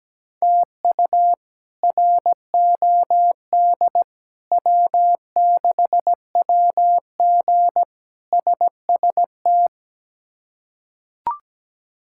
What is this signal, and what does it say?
Morse code, heard as a keyed tone, about 17 wpm, 700 Hz. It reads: TU ROD W6WG SST
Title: K1USN SST QSO Practice - 17wpm